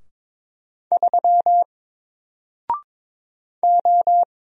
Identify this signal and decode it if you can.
Morse code, heard as a keyed tone, about 22 wpm, 700 Hz. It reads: 3 O
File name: Single-Letter-Number-22wpm